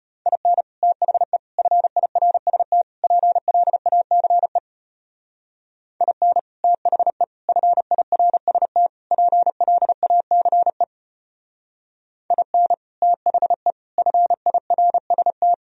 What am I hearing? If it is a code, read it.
Morse code, heard as a keyed tone, about 31 wpm, 700 Hz. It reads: IN THE FIRST PLACE IN THE FIRST PLACE IN THE FIRST
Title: Speed-Racing - Sets of 4 Words 25wpm